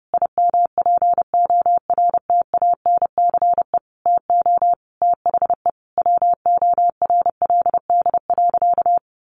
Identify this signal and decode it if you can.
Morse code, heard as a keyed tone, about 30 wpm, 700 Hz. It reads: IMPORTANCE TO THE WORLD.
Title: Sentences from Top 1200 Words 30wpm